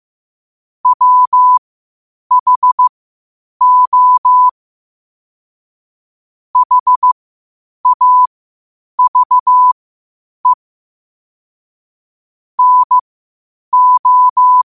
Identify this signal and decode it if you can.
Morse code, heard as a keyed tone, about 15 wpm, spaced as for 8 wpm, 1000 Hz. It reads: WHO HAVE NO
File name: qotdfiles / 8wpm